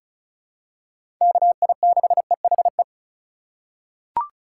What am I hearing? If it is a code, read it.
Morse code, heard as a keyed tone, about 36 wpm, 700 Hz. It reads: KI6EHE
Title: USA Call Signs - 6 Characters - 4x Character Spacing - 35wpm